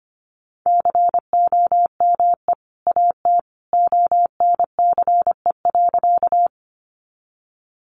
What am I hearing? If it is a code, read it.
Morse code, heard as a keyed tone, about 25 wpm, 700 Hz. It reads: COME AT ONCE.
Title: Sentences from Top 300 Words 25wpm